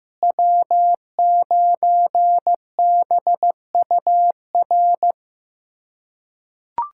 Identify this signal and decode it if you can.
Morse code, heard as a keyed tone, about 15 wpm, 700 Hz. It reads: W9BUR
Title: USA Callsign DE Callsign 15wpm